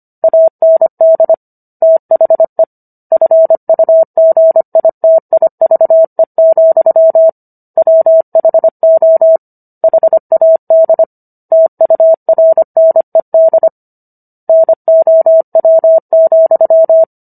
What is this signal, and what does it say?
Morse code, heard as a keyed tone, about 25 wpm, 650 Hz. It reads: AND THE FUGITIVE, WHO HAD TURNED NOW,